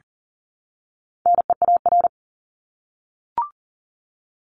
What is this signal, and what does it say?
Morse code, heard as a keyed tone, about 40 wpm, 700 Hz. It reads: NEAR
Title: Top 200 Words Repeated 40wpm